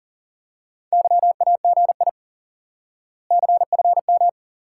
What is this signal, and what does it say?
Morse code, heard as a keyed tone, about 38 wpm, 700 Hz. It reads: YAGI CFM
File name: IQE Course 3 - Lesson 89 - YAGI - Two Elements - 40wpm